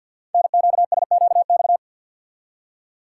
Morse code, heard as a keyed tone, about 52 wpm, 700 Hz. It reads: NQSQX
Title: Code Groups - 5 Letters - 50wpm